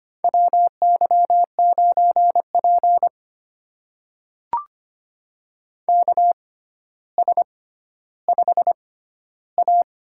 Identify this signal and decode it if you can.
Morse code, heard as a keyed tone, about 25 wpm, spaced as for 13 wpm, 700 Hz. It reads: WY9P KS5A
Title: USA Call Signs - 4 Characters - 6x Character Spacing - 25wpm